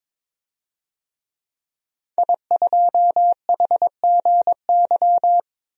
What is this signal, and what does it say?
Morse code, heard as a keyed tone, about 22 wpm, 700 Hz. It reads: I2HGY
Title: DXCC-100 Call Signs - 5 Characters - 22wpm